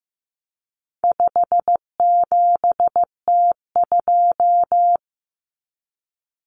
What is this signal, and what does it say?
Morse code, heard as a keyed tone, about 15 wpm, 700 Hz. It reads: 57T2
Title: Common Cut Numbers - 4-Digit 15wpm